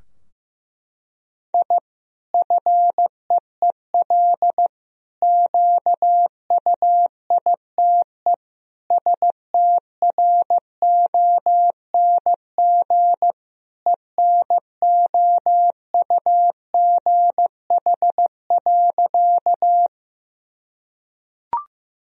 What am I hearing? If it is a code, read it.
Morse code, heard as a keyed tone, about 16 wpm, 700 Hz. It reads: I FEEL QUITE STRONG ENOUGH.
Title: Sentences from Top 1000 Words - Review Part 3 - 15wpm